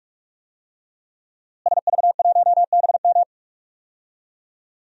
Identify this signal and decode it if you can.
Morse code, heard as a keyed tone, about 41 wpm, 700 Hz. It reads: SV1BM
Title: DXCC-100 Call Signs - 5 Characters - Repeated 3x - 45wpm